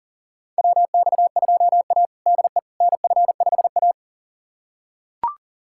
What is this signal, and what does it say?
Morse code, heard as a keyed tone, about 38 wpm, 700 Hz. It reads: WX2A DE NF5A
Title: USA Callsign DE Callsign - 4 Characters - 40wpm